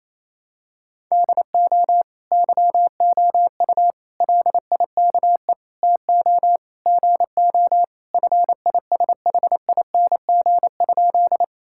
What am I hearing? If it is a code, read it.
Morse code, heard as a keyed tone, about 28 wpm, 700 Hz. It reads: DO YOU LIKE TO GO FISHING?
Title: Sentences from Top 1000 Words - Review Part 5 - 28wpm